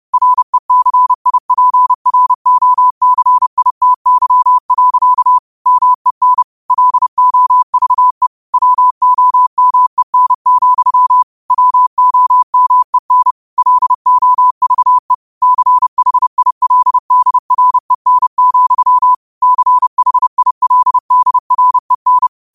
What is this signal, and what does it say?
Morse code, heard as a keyed tone, about 30 wpm, 1000 Hz. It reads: RECIPROCITY. MEN LOVE WOMEN, WOMEN LOVE CHILDREN, CHILDREN